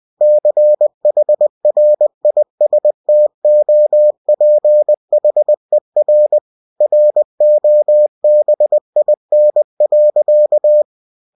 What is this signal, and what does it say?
Morse code, heard as a keyed tone, about 20 wpm, 600 Hz. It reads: CHRISTOPHER ROBIN.
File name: cw-books / winnie-the-pooh / winnie-20wpm-page-0110